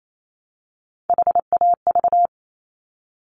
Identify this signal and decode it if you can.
Morse code, heard as a keyed tone, about 28 wpm, 700 Hz. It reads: HAV
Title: Top Three Letter Combinations in Words 28wpm